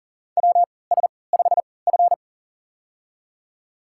Morse code, heard as a keyed tone, about 40 wpm, spaced as for 23 wpm, 700 Hz. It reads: WS5F
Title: USA Call Signs - 3x Character Spacing - 40wpm